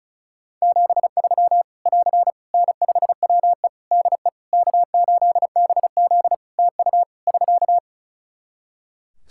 Morse code, heard as a keyed tone, about 36 wpm, 700 Hz. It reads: 73 <AR> N5WE DE K8BZ TU <SK>
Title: K1USN SST QSO Practice 35wpm